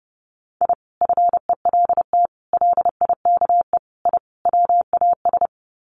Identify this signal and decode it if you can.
Morse code, heard as a keyed tone, about 30 wpm, 700 Hz. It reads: I FELT LIKE I WAS
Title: Sets of 5 Words Encore 30wpm